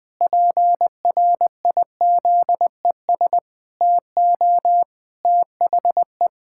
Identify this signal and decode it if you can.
Morse code, heard as a keyed tone, about 20 wpm, 700 Hz. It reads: PRIZES TO THE
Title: Pangram Challenge v2 - 20wpm